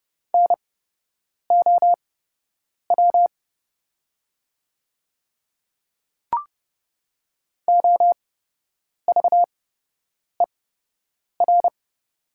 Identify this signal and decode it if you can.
Morse code, heard as a keyed tone, about 30 wpm, spaced as for 8 wpm, 700 Hz. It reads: NOW OVER
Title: 8x Character Spacing - Top 200 Words - 30wpm